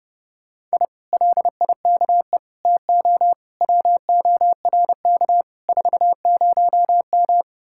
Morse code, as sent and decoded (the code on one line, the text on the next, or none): .. / .-.. .. -.- . / - --- / .-- --- .-. -.- / ....- ----- --
I LIKE TO WORK 40M